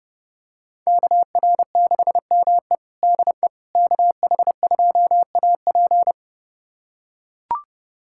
Morse code, heard as a keyed tone, about 32 wpm, 700 Hz. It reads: KR6ME DE KH2AP
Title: USA Callsign DE Callsign - 5 Characters - 30wpm